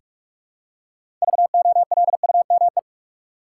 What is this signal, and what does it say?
Morse code, heard as a keyed tone, about 46 wpm, 700 Hz. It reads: VOLUME
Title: Six Letter Words - 45wpm